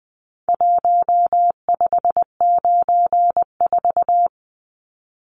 Morse code, heard as a keyed tone, about 20 wpm, 700 Hz. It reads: 1594
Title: Numbers - Contest Style - 4-Digit 20wpm